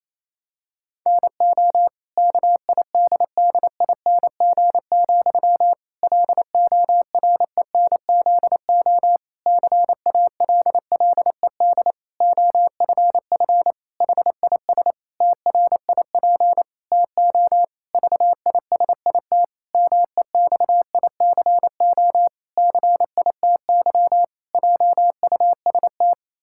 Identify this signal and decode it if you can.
Morse code, heard as a keyed tone, about 28 wpm, 700 Hz. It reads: NO KIDDING, LORENZO CALLED OFF HIS TRIP TO VISIT MEXICO CITY JUST
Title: Pangram Challenge v2 - 28wpm